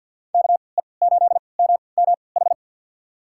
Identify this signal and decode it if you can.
Morse code, heard as a keyed tone, about 47 wpm, spaced as for 28 wpm, 700 Hz. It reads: KE8MMH
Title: USA Call Signs - 6 Characters - 3x Character Spacing - 50wpm